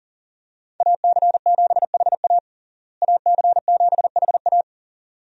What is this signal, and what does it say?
Morse code, heard as a keyed tone, about 40 wpm, 700 Hz. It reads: AC7HA AC7HA
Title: USA Call Signs - 5 Characters - Repeated 2x - 40wpm